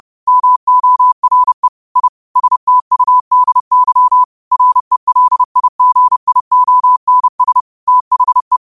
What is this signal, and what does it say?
Morse code, heard as a keyed tone, about 30 wpm, 1000 Hz. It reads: MORE I STUDY RELIGIONS THE
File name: qotdfiles / 30wpm